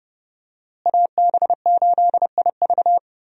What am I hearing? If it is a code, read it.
Morse code, heard as a keyed tone, about 30 wpm, 700 Hz. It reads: AB8IV
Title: USA Call Signs - 5 Characters - Repeated 3x - 30wpm